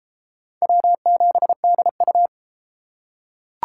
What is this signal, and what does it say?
Morse code, heard as a keyed tone, about 28 wpm, 700 Hz. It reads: W7DU